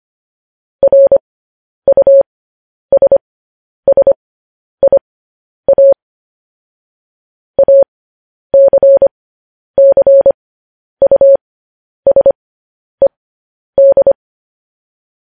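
Morse code, heard as a keyed tone, about 25 wpm, spaced as for 10 wpm, 550 Hz. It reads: RUSSIA ACCUSED
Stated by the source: Morse transmission of news headlines.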